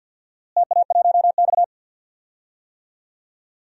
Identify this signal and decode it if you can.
Morse code, heard as a keyed tone, about 53 wpm, 700 Hz. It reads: TA1X
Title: DXCC-100 Call Signs - 4 Characters - Repeated 2x - 50wpm